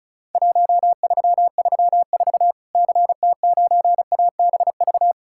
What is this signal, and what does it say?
Morse code, heard as a keyed tone, about 36 wpm, 700 Hz. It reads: 1334 CT9ABV
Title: Worked All Europe DX Contest - QTC Exchange - 35wpm